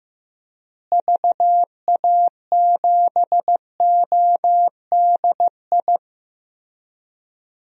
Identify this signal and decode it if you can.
Morse code, heard as a keyed tone, about 15 wpm, 700 Hz. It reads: VA7ODI
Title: DXCC-100 Call Signs - 6 Characters - Repeated 2x - 15wpm